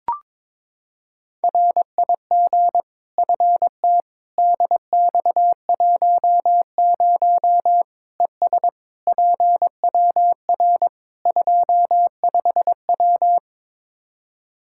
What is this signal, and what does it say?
Morse code, heard as a keyed tone, about 22 wpm, 700 Hz. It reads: RIG FT DX10 ES PWR 25W
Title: Practice QSO - Exchange 2 - Info - 22wpm